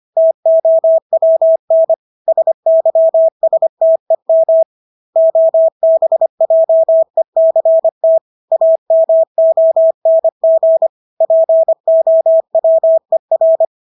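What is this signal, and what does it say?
Morse code, heard as a keyed tone, about 25 wpm, 650 Hz. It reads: TOWN SYSTEM OBJECT AMONG POWER
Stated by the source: Morse practice of English words at 25 wpm from CW med Gnister.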